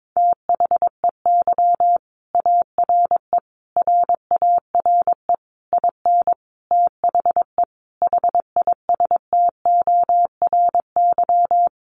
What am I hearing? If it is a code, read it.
Morse code, heard as a keyed tone, about 22 wpm, 700 Hz. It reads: THEY ARE RARE IN THE HISTORY
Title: Sentences from Top 1100 Words 22wpm